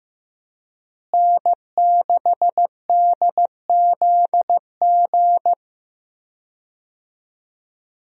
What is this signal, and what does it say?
Morse code, heard as a keyed tone, about 15 wpm, 700 Hz. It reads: N6DZG
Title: USA Call Signs - 5 Characters - Repeated 3x - 15wpm